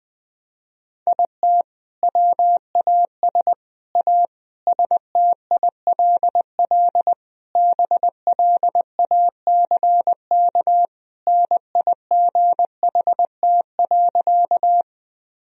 Morse code, heard as a keyed tone, about 20 wpm, 700 Hz. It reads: IT WAS A STILL BLACK NIGHT.
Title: Sentences from Top 1000 Words - Review Part 2 - 20wpm